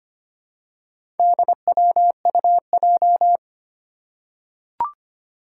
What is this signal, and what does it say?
Morse code, heard as a keyed tone, about 25 wpm, 700 Hz. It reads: DWUJ
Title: Four Challenging Letter-Number Combinations 25wpm